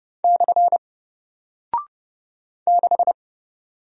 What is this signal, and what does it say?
Morse code, heard as a keyed tone, about 29 wpm, 700 Hz. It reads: / 6
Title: Single Character (Letter:Number:Punctuation:Prosign) Alt 30wpm